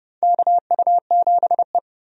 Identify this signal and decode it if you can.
Morse code, heard as a keyed tone, about 30 wpm, 700 Hz. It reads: KU7E
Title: USA Call Signs - 4 Characters - 30wpm